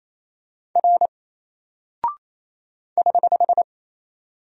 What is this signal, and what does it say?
Morse code, heard as a keyed tone, about 27 wpm, 700 Hz. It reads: R <HH>
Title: Single Character (Letter:Number:Punctuation:Prosign) 28wpm